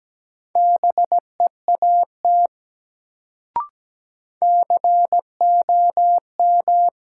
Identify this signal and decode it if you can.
Morse code, heard as a keyed tone, about 17 wpm, 700 Hz. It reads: BEAT COM
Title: Sentences from Top 1300 Words - New Word Drill - 17wpm